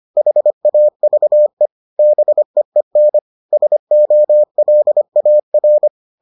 Morse code, heard as a keyed tone, about 25 wpm, 600 Hz. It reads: HAVE BEEN SOLAR